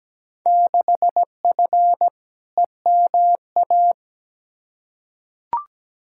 Morse code, heard as a keyed tone, about 17 wpm, 700 Hz. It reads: 6F EMA
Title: Field Day Exchanges Encore - 17wpm